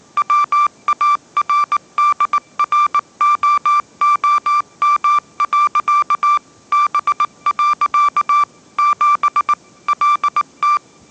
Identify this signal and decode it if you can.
Morse code, heard as a keyed tone, about 19 wpm, 1180 Hz. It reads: WARDROOM. B. 7 LT